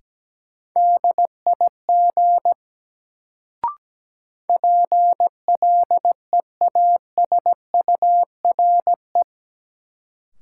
Morse code, heard as a keyed tone, about 17 wpm, 700 Hz. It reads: DIG PLEASURE
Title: Sentences from Top 1600 Words - New Word Drill Alt - 17wpm